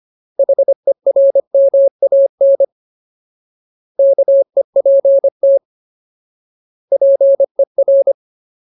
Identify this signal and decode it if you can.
Morse code, heard as a keyed tone, about 25 wpm, 550 Hz. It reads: HERMAN KEPT PER
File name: Ultimate Word List / 25WPM Ultimate